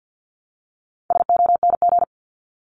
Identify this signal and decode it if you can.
Morse code, heard as a keyed tone, about 50 wpm, 700 Hz. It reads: SONG
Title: Top 500 Words 50wpm